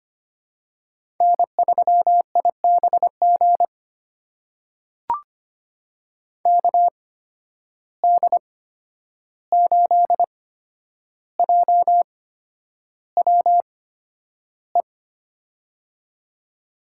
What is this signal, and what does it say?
Morse code, heard as a keyed tone, about 25 wpm, spaced as for 10 wpm, 700 Hz. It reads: N3IBG KD8JWE
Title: USA Call Signs - 8x Character Spacing - 25wpm